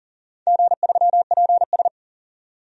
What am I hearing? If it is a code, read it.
Morse code, heard as a keyed tone, about 40 wpm, 690 Hz. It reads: G3PS
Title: DXCC-100 Call Signs - 4 Characters - 40wpm